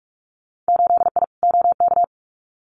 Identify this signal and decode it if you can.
Morse code, heard as a keyed tone, about 44 wpm, 700 Hz. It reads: 8I OK
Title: Winter Field Day Exchanges Encore 45wpm